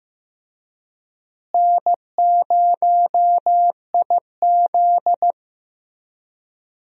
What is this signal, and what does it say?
Morse code, heard as a keyed tone, about 15 wpm, 700 Hz. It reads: N0IZ
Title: USA Call Signs - 4 Characters - Repeated 3x - 15wpm